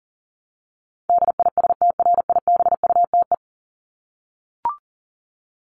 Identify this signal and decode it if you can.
Morse code, heard as a keyed tone, about 38 wpm, 700 Hz. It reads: DISTRIBUTE
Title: Top 1501-1750 Words 40wpm